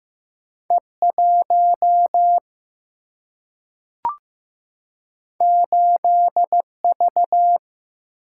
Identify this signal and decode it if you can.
Morse code, heard as a keyed tone, about 15 wpm, 700 Hz. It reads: E1 8V
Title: Single-Letter Single-Number 15wpm